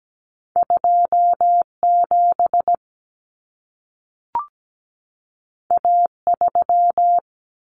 Morse code, heard as a keyed tone, about 17 wpm, 700 Hz. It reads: 27 A3
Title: Common Cut Numbers - 2-Digit 17wpm